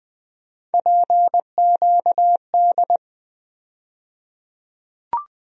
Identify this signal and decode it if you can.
Morse code, heard as a keyed tone, about 20 wpm, 700 Hz. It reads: PQD
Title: USA Call Sign Postfixes - 3 Letters 20wpm